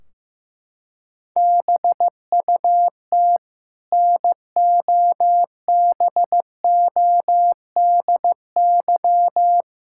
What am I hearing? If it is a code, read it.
Morse code, heard as a keyed tone, about 15 wpm, 700 Hz. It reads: BUT NOBODY
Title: Sentences from Top 1600 Words 15wpm